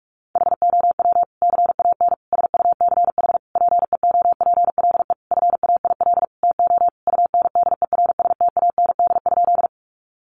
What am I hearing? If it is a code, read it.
Morse code, heard as a keyed tone, about 44 wpm, 700 Hz. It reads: HOW CAN SUCH PEOPLE FAIL TO UNDERSTAND?
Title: Sentences from Top 1100 Words 45wpm